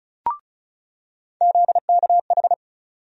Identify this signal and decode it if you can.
Morse code, heard as a keyed tone, about 36 wpm, 700 Hz. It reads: ZKH
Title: USA Call Sign Postfixes - 3 Letters 35wpm